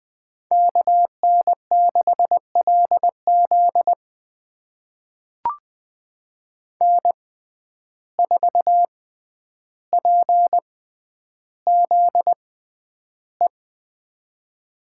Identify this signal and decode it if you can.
Morse code, heard as a keyed tone, about 20 wpm, spaced as for 10 wpm, 700 Hz. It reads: KN6LZ N4PZE
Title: USA Call Signs - 5 Characters - 6x Character Spacing - 20wpm